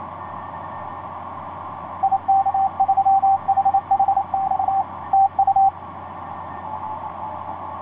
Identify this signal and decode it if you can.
Morse code, heard as a keyed tone, about 28 wpm, 780 Hz, with radio noise. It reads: IK3HHX TU
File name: 6w-gm4fdm f6bee etc 15m cw 30jan2010 1502